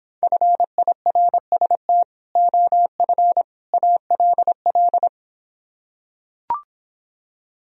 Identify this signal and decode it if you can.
Morse code, heard as a keyed tone, about 26 wpm, 700 Hz. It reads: FIRST OF ALL